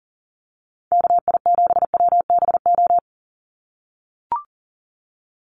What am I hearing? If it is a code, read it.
Morse code, heard as a keyed tone, about 40 wpm, 700 Hz. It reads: KI7WBO
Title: USA Call Signs 40wpm